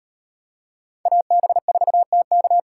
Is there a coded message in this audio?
Yes: Morse code from 1.1 to 2.6 s, heard as a keyed tone, about 36 wpm, 700 Hz. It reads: AB4TK